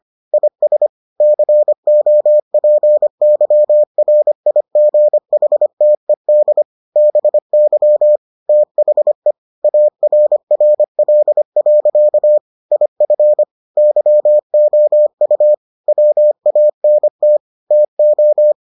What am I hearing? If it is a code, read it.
Morse code, heard as a keyed tone, about 25 wpm, 600 Hz. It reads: IS COPYRIGHTED BY THE ARRL. IF YOU WANT TO